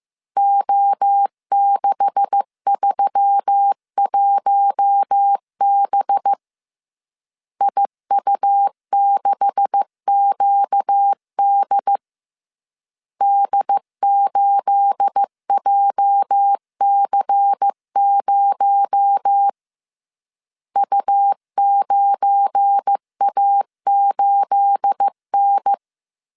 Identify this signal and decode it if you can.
Morse code, heard as a keyed tone, about 15 wpm, 800 Hz. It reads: O631B IU6QD D8JC0 U9A8N
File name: CW / DET / 1 9 1500C S C 14wpm 3245